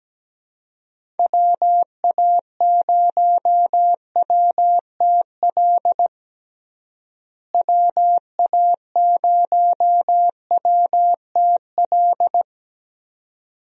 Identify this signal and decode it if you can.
Morse code, heard as a keyed tone, about 17 wpm, 700 Hz. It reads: WA0WTL WA0WTL
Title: USA Call Signs - 6 Characters - Repeated 2x - 17wpm